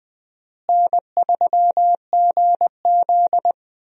Morse code, heard as a keyed tone, about 20 wpm, 700 Hz. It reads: N3GZ
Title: USA Call Signs - 4 Characters - Repeated 3x - 20wpm